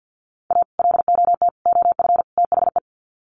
Morse code, heard as a keyed tone, about 50 wpm, 700 Hz. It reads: A LOT OF THE
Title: Sets of 4 Words 50wpm